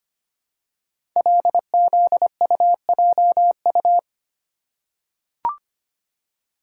Morse code, heard as a keyed tone, about 25 wpm, 700 Hz. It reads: LZUJU
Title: Code Groups - 5 Letters - 25wpm